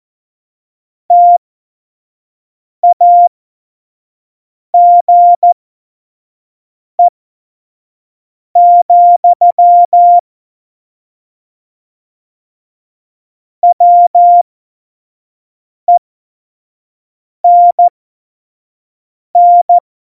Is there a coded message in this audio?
Yes: Morse code from 1.1 to 19.8 s, heard as a keyed tone, about 14 wpm, spaced as for 5 wpm, 700 Hz. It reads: TAGE, WENN